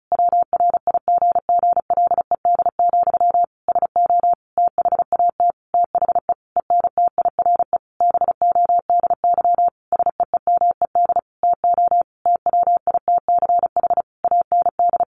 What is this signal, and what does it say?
Morse code, heard as a keyed tone, about 34 wpm, 700 Hz. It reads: WRIGGLED, SO THAT THE ENTIRE BODY SEEMED TO TWITCH AND
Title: A Sound Of Thunder 35wpm